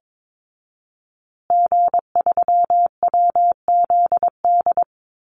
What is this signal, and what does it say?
Morse code, heard as a keyed tone, about 22 wpm, 700 Hz. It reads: G3WZD
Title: Call Signs Repeated 2x - 22wpm